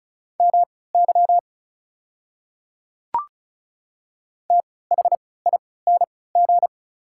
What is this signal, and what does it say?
Morse code, heard as a keyed tone, about 34 wpm, spaced as for 20 wpm, 690 Hz. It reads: MY THING
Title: 3x Character Spacing - Top 100 Words - 35wpm